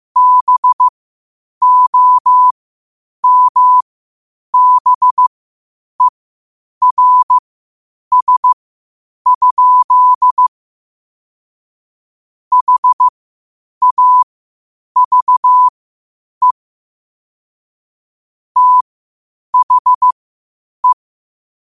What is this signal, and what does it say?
Morse code, heard as a keyed tone, about 15 wpm, spaced as for 8 wpm, 1000 Hz. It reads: BOMBERS? HAVE THE
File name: qotdfiles / 8wpm